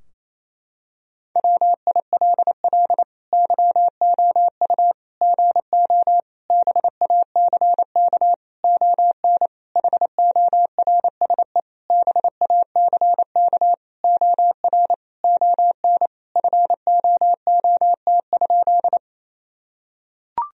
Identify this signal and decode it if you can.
Morse code, heard as a keyed tone, about 28 wpm, 700 Hz. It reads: WILL YOU GO BACK ON HORSE BACK OR ON FOOT?
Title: Sentences from Top 1000 Words - Review Part 3 - 28wpm